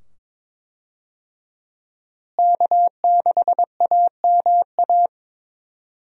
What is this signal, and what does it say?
Morse code, heard as a keyed tone, about 22 wpm, 700 Hz. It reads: K6AMA
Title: USA Call Signs - 5 Characters - Repeated 2x - 22wpm